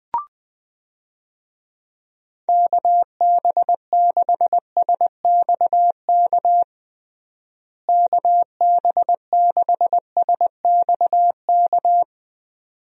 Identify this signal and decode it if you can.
Morse code, heard as a keyed tone, about 20 wpm, 700 Hz. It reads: KB6SXK KB6SXK
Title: USA Call Signs - 6 Characters - Repeated 2x - 20wpm